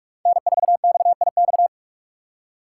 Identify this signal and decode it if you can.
Morse code, heard as a keyed tone, about 45 wpm, 700 Hz. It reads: N4XIX
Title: USA Call Signs - 3x Character Spacing - 45wpm